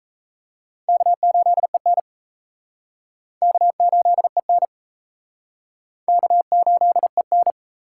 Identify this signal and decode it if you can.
Morse code, heard as a keyed tone, about 34 wpm, 700 Hz. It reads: K8EN K8EN K8EN
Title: Speed Racing - USA 1x2 Call Signs 28wpm